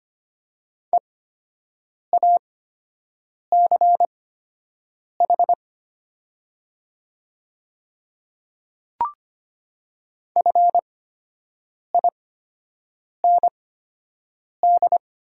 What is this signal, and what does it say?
Morse code, heard as a keyed tone, about 25 wpm, spaced as for 7 wpm, 700 Hz. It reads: EACH FIND
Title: 8x Character Spacing - Top 100 Words - 25wpm